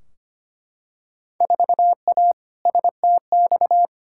Morse code, heard as a keyed tone, about 25 wpm, 700 Hz. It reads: 4A STX
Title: Field Day Exchanges - 25wpm